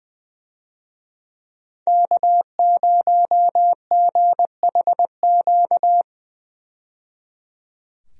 Morse code, heard as a keyed tone, about 20 wpm, 700 Hz. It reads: K0GHQ
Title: USA Call Signs - 5 Characters - 20wpm